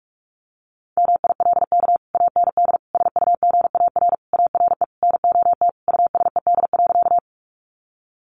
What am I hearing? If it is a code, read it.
Morse code, heard as a keyed tone, about 44 wpm, 700 Hz. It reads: MILK AND SUGAR ARE NOT USED.
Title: Sentences from Top 1500 Words 45wpm